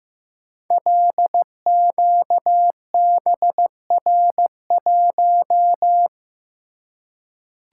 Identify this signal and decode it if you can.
Morse code, heard as a keyed tone, about 15 wpm, 700 Hz. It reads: LQBR1
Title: Code Groups - 5 Letter:Numbers - 15wpm